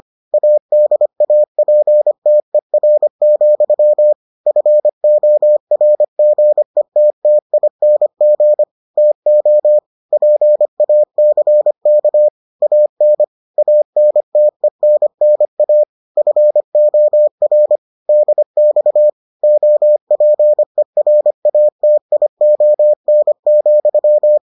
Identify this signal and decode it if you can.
Morse code, heard as a keyed tone, about 25 wpm, 600 Hz. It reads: ADAPTER, FORGETTING TO PACK AN ANTENNA FOR DX OPERATION,